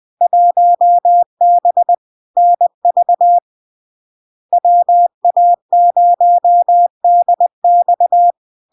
Morse code, heard as a keyed tone, about 20 wpm, 700 Hz. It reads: 1B NV WA0DX
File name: LARK-FD-2022-Log-in-CW-20wpm-700-Hz